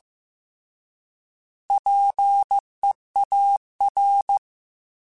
Morse code, heard as a keyed tone, about 15 wpm, 800 Hz. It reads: PEAR